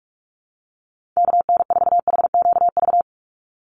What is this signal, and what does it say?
Morse code, heard as a keyed tone, about 44 wpm, 700 Hz. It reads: KN4HQV
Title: USA Call Signs 45wpm